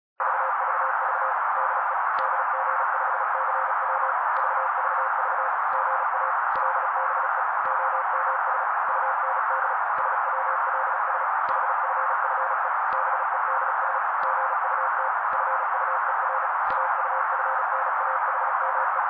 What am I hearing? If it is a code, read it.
Morse code, heard as a keyed tone, about 35 wpm, 530 Hz. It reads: VVV DE DG5CW VVV GA LBE OMS UND SWLS HFFE BEDS GUT FUER AUFNAHME
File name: daily / 35wpm noise